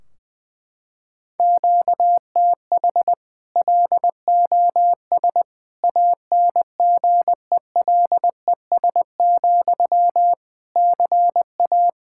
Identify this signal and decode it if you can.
Morse code, heard as a keyed tone, about 20 wpm, 700 Hz. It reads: QTH LOS ANGELES, CA
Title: Practice QSO - Exchange 1 - Beginning - 20wpm